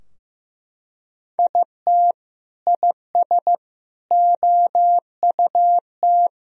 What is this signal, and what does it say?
Morse code, heard as a keyed tone, about 15 wpm, 700 Hz. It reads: IT IS OUT
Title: Sentences from Top 1000 Words - Review Part 4 - 15wpm